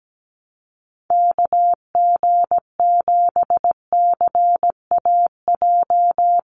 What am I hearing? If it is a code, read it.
Morse code, heard as a keyed tone, about 17 wpm, 700 Hz. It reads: KG7CAJ
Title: USA Call Signs 17wpm